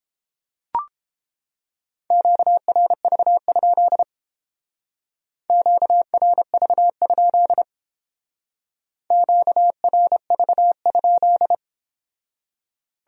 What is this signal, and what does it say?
Morse code, heard as a keyed tone, about 30 wpm, 700 Hz. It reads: QRV? QRV? QRV?